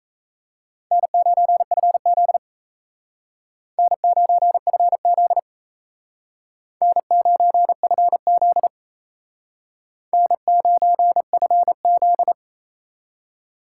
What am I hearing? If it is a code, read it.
Morse code, heard as a keyed tone, about 36 wpm, 700 Hz. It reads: N9FZ N9FZ N9FZ N9FZ